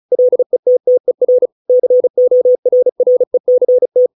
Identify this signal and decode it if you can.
Morse code, heard as a keyed tone, about 36 wpm, 500 Hz. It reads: LETTER CORRECT